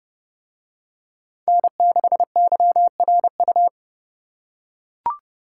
Morse code, heard as a keyed tone, about 28 wpm, 700 Hz. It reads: N6YRU